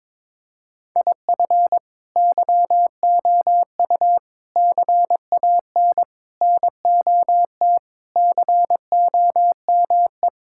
Morse code, heard as a keyed tone, about 22 wpm, 700 Hz. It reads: IF YOU CAN NOT COME
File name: practice / 22wpm / Sentences with 4-Word Phrases - 22wpm